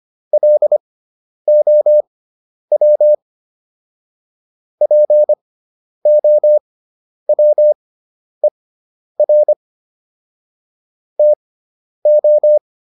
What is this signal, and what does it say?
Morse code, heard as a keyed tone, about 25 wpm, spaced as for 10 wpm, 600 Hz. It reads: LOW POWER TO